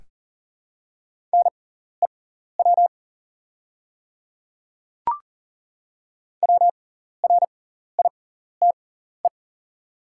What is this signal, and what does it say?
Morse code, heard as a keyed tone, about 40 wpm, spaced as for 14 wpm, 700 Hz. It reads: NEW WRITE
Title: 6x Character Spacing - Top 300 Words - 40wpm